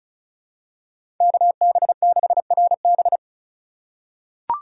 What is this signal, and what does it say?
Morse code, heard as a keyed tone, about 36 wpm, 700 Hz. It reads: KB6RB
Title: USA Call Signs - 5 Characters - 2x Character Spacing - 35wpm